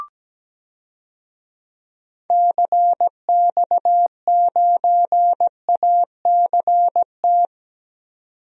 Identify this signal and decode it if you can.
Morse code, heard as a keyed tone, about 17 wpm, 700 Hz. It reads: CX9ACT
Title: DXCC-100 Call Signs - 6 Characters - 17wpm